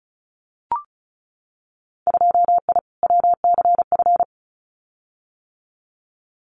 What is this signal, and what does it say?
Morse code, heard as a keyed tone, about 36 wpm, 700 Hz. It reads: 2I WCF
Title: Winter Field Day Exchanges 35wpm